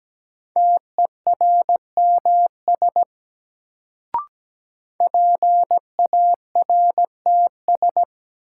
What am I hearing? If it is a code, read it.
Morse code, heard as a keyed tone, about 17 wpm, 700 Hz. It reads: TERMS PARTS
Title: Sets of 3 Words - New Word Drill - 17wpm